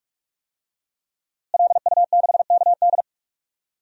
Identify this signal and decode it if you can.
Morse code, heard as a keyed tone, about 43 wpm, 700 Hz. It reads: LU6KD
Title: DXCC-100 Call Signs - 5 Characters - Repeated 2x - 45wpm